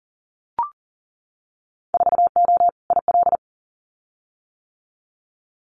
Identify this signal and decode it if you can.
Morse code, heard as a keyed tone, about 40 wpm, 700 Hz. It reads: 4O IL
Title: Winter Field Day Exchanges Encore 40wpm